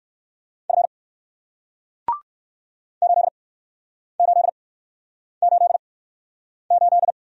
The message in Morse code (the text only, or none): F 8888